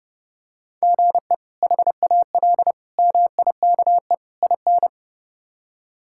Morse code, heard as a keyed tone, about 30 wpm, 700 Hz. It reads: GE HAL MIKE IN
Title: K1USN SST QSO Practice - 30wpm